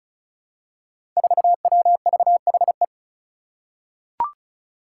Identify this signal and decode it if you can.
Morse code, heard as a keyed tone, about 34 wpm, 700 Hz. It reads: 4WVHE
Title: Five Challenging Letter-Number Combinations 35wpm